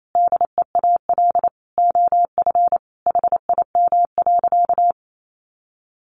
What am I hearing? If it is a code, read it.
Morse code, heard as a keyed tone, about 28 wpm, 700 Hz. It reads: DEAL OF HIM.
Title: Sentences from Top 800 Words 28wpm